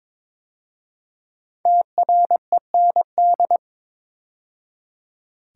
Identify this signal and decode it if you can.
Morse code, heard as a keyed tone, about 22 wpm, 700 Hz. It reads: TREND
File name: 5 Letter Words - First 3 Letters Given - 22wpm